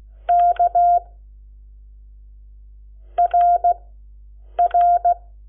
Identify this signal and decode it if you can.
Morse code, heard as a keyed tone, about 16 wpm, spaced as for 8 wpm, 700 Hz. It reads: K RR
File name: Lecciones / CLASE Nº7